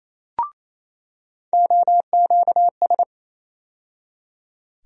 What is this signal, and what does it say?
Morse code, heard as a keyed tone, about 28 wpm, 700 Hz. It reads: OQS
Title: USA Call Sign Postfixes - 3 Letters 28wpm